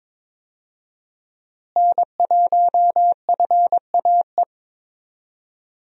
Morse code, heard as a keyed tone, about 22 wpm, 700 Hz. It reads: N1FAE
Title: USA Call Signs - 5 Characters - Repeated 3x - 22wpm